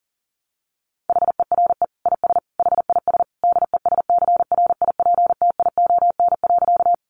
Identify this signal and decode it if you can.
Morse code, heard as a keyed tone, about 40 wpm, 700 Hz. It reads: HERE IS HIS DESCRIPTION.
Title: Sentences from Top 1300 Words 40wpm